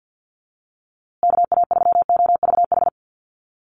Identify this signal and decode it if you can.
Morse code, heard as a keyed tone, about 48 wpm, 700 Hz. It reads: KA3OVH
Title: USA Call Signs Repeated 2x 50wpm